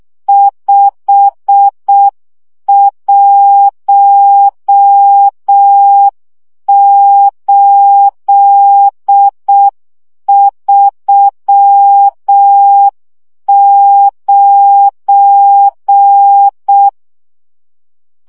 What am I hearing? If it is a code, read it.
Morse code, heard as a keyed tone, about 6 wpm, 800 Hz. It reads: 51839
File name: morze vizsga 2011-3 5